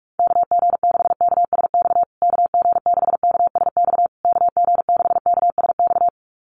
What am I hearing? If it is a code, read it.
Morse code, heard as a keyed tone, about 44 wpm, 700 Hz. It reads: KG6KSX KG6KSX KG6KSX
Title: USA Call Signs Repeated 3x 45wpm